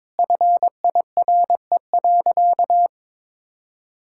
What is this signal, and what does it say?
Morse code, heard as a keyed tone, about 22 wpm, 700 Hz. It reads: FIRE.
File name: Sentences with 9-Word Phrases - 22wpm